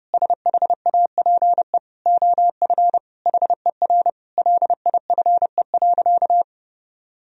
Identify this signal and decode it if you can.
Morse code, heard as a keyed tone, about 30 wpm, 700 Hz. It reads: SHAPE OF HER LIFE.
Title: Sentences from Top 1000 Words - Review Part 3 - 30wpm